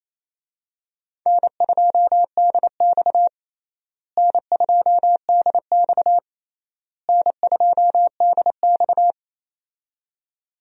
Morse code, heard as a keyed tone, about 28 wpm, 700 Hz. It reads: N2DX N2DX N2DX
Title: USA Call Signs - 4 Characters - Repeated 3x - 28wpm